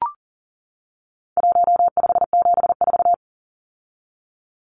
Morse code, heard as a keyed tone, about 40 wpm, 700 Hz. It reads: 1574
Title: Numbers - 4-Digit 40wpm